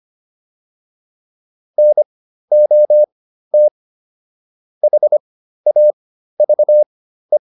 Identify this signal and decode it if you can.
Morse code, heard as a keyed tone, about 25 wpm, spaced as for 13 wpm, 600 Hz. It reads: NOT HAVE